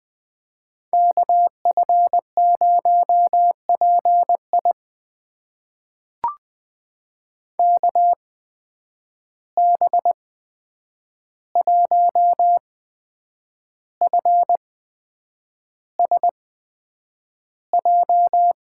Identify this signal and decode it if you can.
Morse code, heard as a keyed tone, about 20 wpm, spaced as for 8 wpm, 700 Hz. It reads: KF0PI KB1FSJ
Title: USA Call Signs - 8x Character Spacing - 20wpm